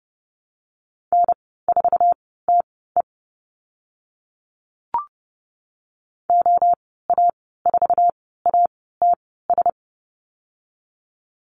Morse code, heard as a keyed tone, about 30 wpm, spaced as for 17 wpm, 700 Hz. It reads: N4TE OA4ATS
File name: Lesson 15g - Number 5 - Call Signs 30wpm